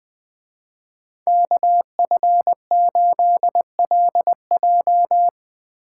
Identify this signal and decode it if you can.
Morse code, heard as a keyed tone, about 22 wpm, 700 Hz. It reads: KF8LJ
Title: Speed Racing - USA 2x2 Call Signs 20wpm